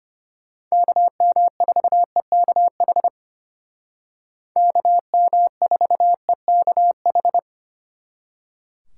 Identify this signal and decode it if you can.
Morse code, heard as a keyed tone, about 32 wpm, 700 Hz. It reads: KM4EKH KM4EKH